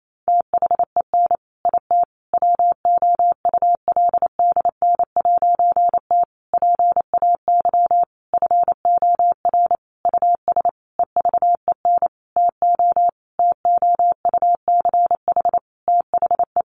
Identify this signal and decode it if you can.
Morse code, heard as a keyed tone, about 28 wpm, 700 Hz. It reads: THEN IT WOULDN'T PAY FOR US EVEN TO TOUCH THE
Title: A Sound Of Thunder 28wpm